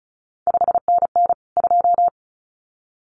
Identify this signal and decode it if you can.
Morse code, heard as a keyed tone, about 36 wpm, 700 Hz. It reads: 5NN 2
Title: CQ WW DX Contest Exchange Encore - 35wpm